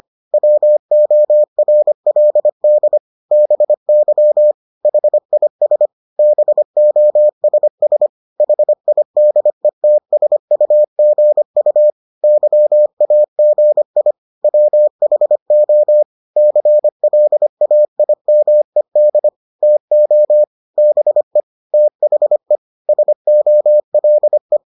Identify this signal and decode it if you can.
Morse code, heard as a keyed tone, about 25 wpm, 600 Hz. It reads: WORLD BY HIS BOSS HIDETSUGU YAGI WHO CLAIMED TO BE THE SOLE